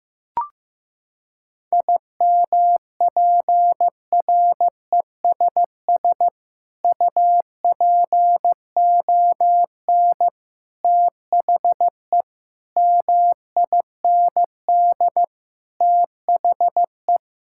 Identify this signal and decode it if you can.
Morse code, heard as a keyed tone, about 15 wpm, 700 Hz. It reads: IMPRESS UPON THE MIND THE
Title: Sentences from Top 1600 Words 15wpm